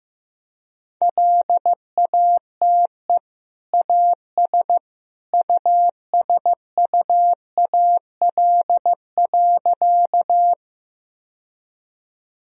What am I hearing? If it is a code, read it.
Morse code, heard as a keyed tone, about 16 wpm, 700 Hz. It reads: LATE AS USUAL.
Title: Sentences from Top 1000 Words - Review Part 2 - 15wpm